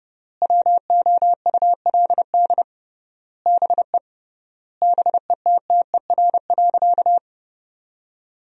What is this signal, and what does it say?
Morse code, heard as a keyed tone, about 30 wpm, 700 Hz. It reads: WOULD BE BETTER.
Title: 3x Word-Spacing - Sentences from Top 400 Words - 30wpm